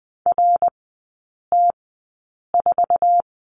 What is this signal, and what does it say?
Morse code, heard as a keyed tone, about 20 wpm, spaced as for 8 wpm, 700 Hz. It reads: RT4
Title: 053 Lesson 10e - Letter r - Three Characters 20wpm